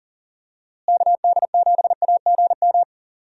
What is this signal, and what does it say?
Morse code, heard as a keyed tone, about 40 wpm, 700 Hz. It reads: KD7AGM
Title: USA Call Signs - 6 Characters - Repeated 2x - 40wpm